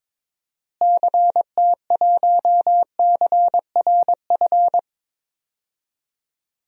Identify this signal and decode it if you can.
Morse code, heard as a keyed tone, about 22 wpm, 700 Hz. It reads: CT1CRF
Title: DXCC-100 Call Signs - 6 Characters - Repeated 2x - 22wpm